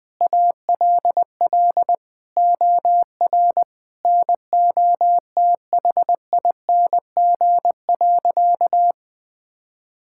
Morse code, heard as a keyed tone, about 20 wpm, 700 Hz. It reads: ALL OR NOTHING.
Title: Sentences from Top 1000 Words - Review Part 6 - 20wpm